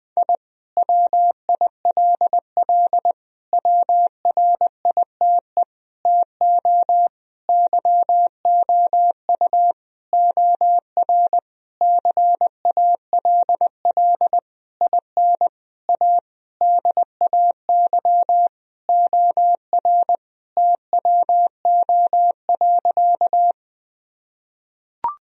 Morse code, heard as a keyed tone, about 20 wpm, 700 Hz. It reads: I WILL WRITE TO YOU OR CALL IN A DAY OR TWO.
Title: Sentences from Top 1000 Words - Review Part 1 - 20wpm